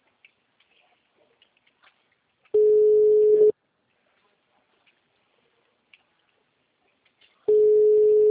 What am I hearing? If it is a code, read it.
Ringback tone (Europe, 425 Hz).
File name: out- -2015- - - .81062